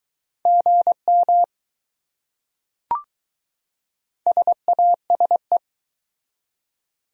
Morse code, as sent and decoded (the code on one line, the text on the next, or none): --. -- / ... .- ... .
GM SASE